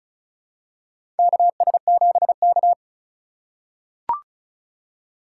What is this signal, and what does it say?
Morse code, heard as a keyed tone, about 36 wpm, 700 Hz. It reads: KS7K
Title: USA Call Signs - 4 Characters - 3x Character Spacing - 35wpm